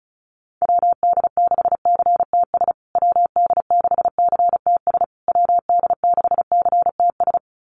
Morse code, heard as a keyed tone, about 34 wpm, 700 Hz. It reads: WD6CTS WD6CTS WD6CTS
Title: USA Call Signs Repeated 3x 35wpm